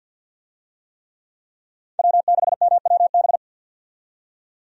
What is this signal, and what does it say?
Morse code, heard as a keyed tone, about 52 wpm, 700 Hz. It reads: W6MWB
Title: USA Call Signs - 5 Characters - 50wpm